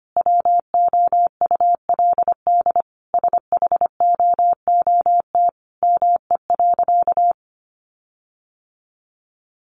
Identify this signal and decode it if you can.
Morse code, heard as a keyed tone, about 25 wpm, 700 Hz. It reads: WOULD SHOOT ME.
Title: Sentences from Top 1500 Words 25wpm